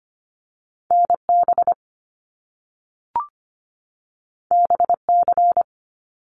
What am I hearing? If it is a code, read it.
Morse code, heard as a keyed tone, about 25 wpm, 700 Hz. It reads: NB BC
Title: Canadian Provinces and Territories 25wpm